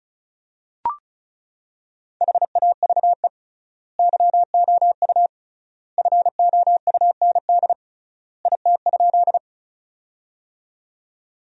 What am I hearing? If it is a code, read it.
Morse code, heard as a keyed tone, about 36 wpm, 700 Hz. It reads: HAVE YOU FOUND IT?
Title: 3x Word-Spacing - Sentences from Top 300 Words - 35wpm